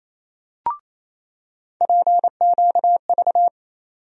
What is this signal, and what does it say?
Morse code, heard as a keyed tone, about 28 wpm, 700 Hz. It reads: PQV
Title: USA Call Sign Postfixes - 3 Letters 28wpm